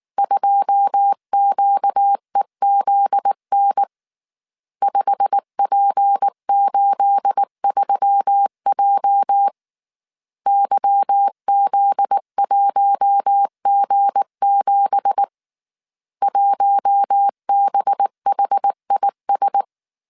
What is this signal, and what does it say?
Morse code, heard as a keyed tone, about 19 wpm, 800 Hz. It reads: 2QEZN 5P83J YZ1G7 1BHIS
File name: CW / DET / 1 9 1500C S A 18wpm 2459